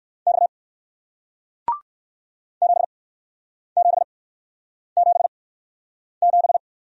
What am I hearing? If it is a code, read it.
Morse code, heard as a keyed tone, about 61 wpm, spaced as for 9 wpm, 700 Hz. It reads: X 7777